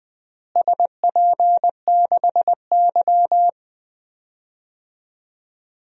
Morse code, heard as a keyed tone, about 20 wpm, 700 Hz. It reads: SP6Y
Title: DXCC-100 Call Signs - 4 Characters - Repeated 3x - 20wpm